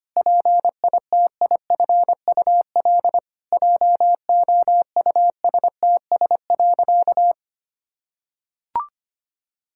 Morse code, heard as a keyed tone, about 25 wpm, 700 Hz. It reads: PITIFUL JOUSTS.
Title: Pangram Challenge v2 - 25wpm